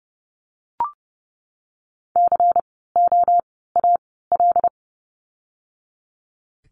Morse code, heard as a keyed tone, about 30 wpm, spaced as for 17 wpm, 700 Hz. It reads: COAL
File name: Lesson 16f - Letter c - Words 30wpm